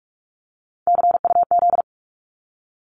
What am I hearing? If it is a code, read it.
Morse code, heard as a keyed tone, about 46 wpm, 700 Hz. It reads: CUZ
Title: Abbreviations 45wpm